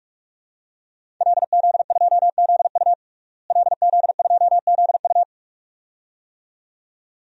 Morse code, heard as a keyed tone, about 46 wpm, 700 Hz. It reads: LZ2ZU LZ2ZU
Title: DXCC-100 Call Signs - 5 Characters - Repeated 2x - 45wpm